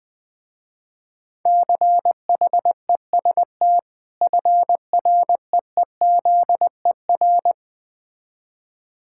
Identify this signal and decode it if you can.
Morse code, heard as a keyed tone, about 20 wpm, 700 Hz. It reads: CHEST FREEZER
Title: Difficult Words - Speed-Racing with Hints - 20wpm